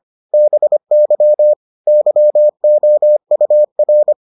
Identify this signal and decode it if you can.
Morse code, heard as a keyed tone, about 25 wpm, 600 Hz. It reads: BY YOUR